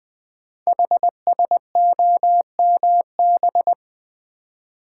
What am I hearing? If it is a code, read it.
Morse code, heard as a keyed tone, about 20 wpm, 700 Hz. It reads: HSOMB
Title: Code Groups - 5 Letters - 20wpm